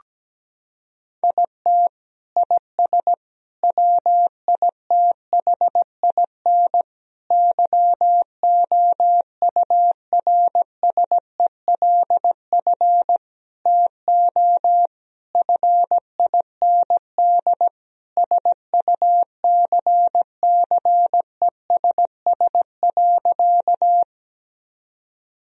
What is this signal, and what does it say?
Morse code, heard as a keyed tone, about 17 wpm, 700 Hz. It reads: IT IS WITHIN YOURSELF TO FIND SUCCESS.
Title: Sentences from Top 1000 Words - Review Part 4 - 17wpm